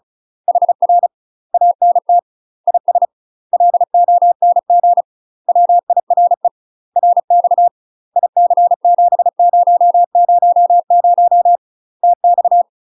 Morse code, heard as a keyed tone, about 36 wpm, 700 Hz. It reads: HR ANT IS LONG WIRE RX IC7000 TX
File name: UK MASTER QSOs TRAINING / 35WPM / UK MASTER QSOs 0031